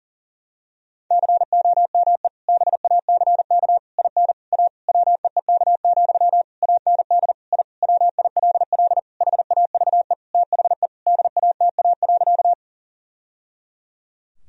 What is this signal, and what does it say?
Morse code, heard as a keyed tone, about 40 wpm, 700 Hz. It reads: COME BACK IN A WEEK, AND I WILL HAVE THE DATA.
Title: Sentences from Top 1000 Words - Review Part 3 - 40wpm